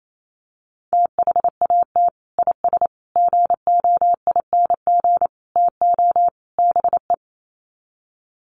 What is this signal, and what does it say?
Morse code, heard as a keyed tone, about 28 wpm, 700 Hz. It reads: THAT IS GOING TO BE
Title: Sets of 5 Words Encore2 28wpm